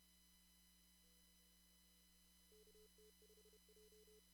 Morse code, heard as a keyed tone, about 30 wpm, 440 Hz, with noise. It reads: KT5J
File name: CW / kt5j wpx cw-op-k5pi